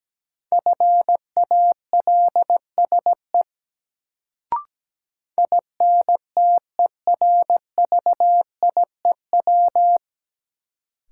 Morse code, heard as a keyed tone, about 17 wpm, 700 Hz. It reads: FALSE INTERVIEW
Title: Sentences from Top 1600 Words - New Word Drill - 17wpm